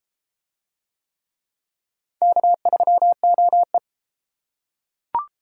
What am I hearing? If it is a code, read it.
Morse code, heard as a keyed tone, about 31 wpm, 700 Hz. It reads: K3OE